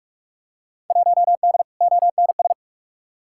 Morse code, heard as a keyed tone, about 45 wpm, 700 Hz. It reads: 1D ONS
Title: Field Day Exchanges - 45wpm